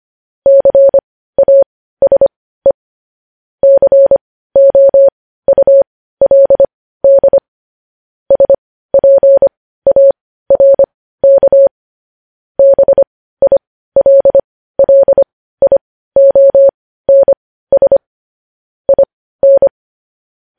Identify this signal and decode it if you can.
Morse code, heard as a keyed tone, about 25 wpm, spaced as for 15 wpm, 550 Hz. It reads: CASE COULD SPARK BILLIONS IN
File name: podcast / News headlines-1x-25x15